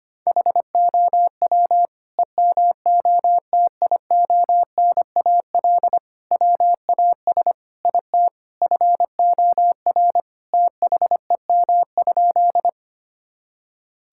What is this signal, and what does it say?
Morse code, heard as a keyed tone, about 25 wpm, 700 Hz. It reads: HOW EMOTIONAL WAS IT FOR THEM?
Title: Sentences from Top 1000 Words - Review Part 5 - 25wpm